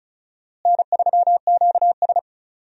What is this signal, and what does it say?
Morse code, heard as a keyed tone, about 36 wpm, 700 Hz. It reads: N3QS
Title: USA Call Signs - 4 Characters - Repeated 3x - 35wpm